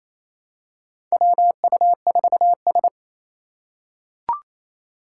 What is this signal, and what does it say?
Morse code, heard as a keyed tone, about 28 wpm, 700 Hz. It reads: WU4S
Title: USA Call Signs - 4 Characters - 2x Character Spacing - 28wpm